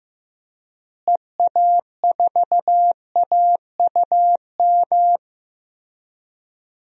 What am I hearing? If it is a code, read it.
Morse code, heard as a keyed tone, about 16 wpm, 700 Hz. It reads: EA4AUM
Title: DXCC-100 Call Signs - 6 Characters - Repeated 3x - 15wpm